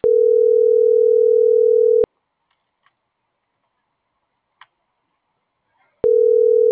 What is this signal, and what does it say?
Ringback tone (North America), 440 + 480 Hz.